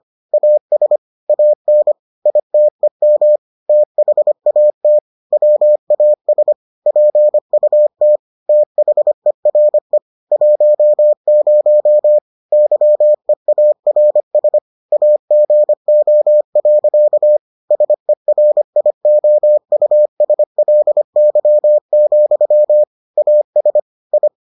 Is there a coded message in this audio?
Yes: Morse code from 0.3 to 24.3 s, heard as a keyed tone, about 25 wpm, 600 Hz. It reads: AS AN ITEM THAT WAS PUT THERE 10 YEARS AGO. SERIOUSLY, AS I